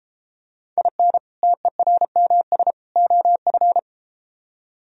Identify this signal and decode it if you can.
Morse code, heard as a keyed tone, about 32 wpm, 700 Hz. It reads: IN TERMS OF